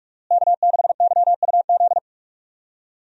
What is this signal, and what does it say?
Morse code, heard as a keyed tone, about 42 wpm, 700 Hz. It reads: K6YUZ
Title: USA Call Signs - 5 Characters - Repeated 2x - 45wpm